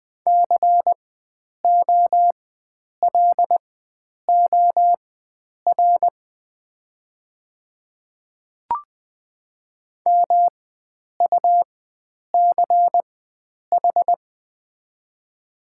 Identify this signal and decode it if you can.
Morse code, heard as a keyed tone, about 20 wpm, spaced as for 9 wpm, 690 Hz. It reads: COLOR MUCH
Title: 4x Character Spacing - Top 300 Words - 20wpm